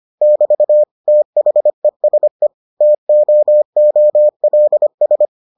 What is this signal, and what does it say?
Morse code, heard as a keyed tone, about 25 wpm, 600 Hz. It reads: <BT> THESE TOOLS